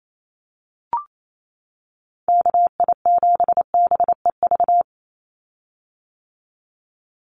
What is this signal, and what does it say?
Morse code, heard as a keyed tone, about 28 wpm, 700 Hz. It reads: KI7BEV
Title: USA Call Signs 28wpm